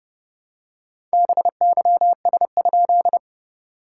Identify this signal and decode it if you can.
Morse code, heard as a keyed tone, about 30 wpm, 690 Hz. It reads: BYS?
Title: SOTA QSO Practice 30wpm